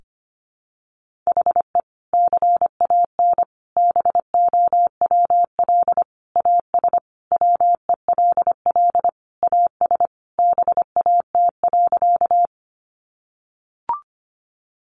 Morse code, heard as a keyed tone, about 25 wpm, 700 Hz. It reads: HE CAN BOWL AS WELL AS BAT.
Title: Sentences from Top 1500 Words 25wpm